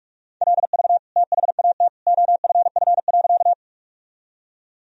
Morse code, heard as a keyed tone, about 41 wpm, 700 Hz. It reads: LV THAT OFF.
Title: Sentences from Top 400 Words and Abbreviations 45wpm